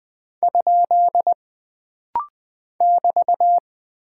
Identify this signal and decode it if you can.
Morse code, heard as a keyed tone, about 20 wpm, 700 Hz. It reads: ? <BT>
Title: Single Punctuation-Prosign - Rapid-Fire - 20wpm